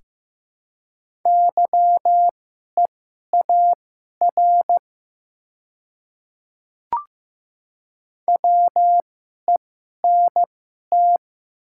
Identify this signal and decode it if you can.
Morse code, heard as a keyed tone, about 15 wpm, spaced as for 11 wpm, 700 Hz. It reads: YEAR WENT
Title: 2x Character Spacing - Top 300 Words - 15wpm